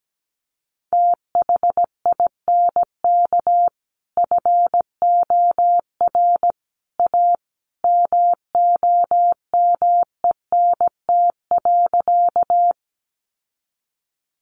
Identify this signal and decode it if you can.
Morse code, heard as a keyed tone, about 17 wpm, 700 Hz. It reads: THINK FOR A MOMENT.
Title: Sentences from Top 1000 Words 17wpm